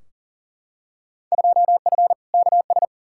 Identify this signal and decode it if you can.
Morse code, heard as a keyed tone, about 40 wpm, 700 Hz. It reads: 2F KS
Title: Field Day Exchanges - 40wpm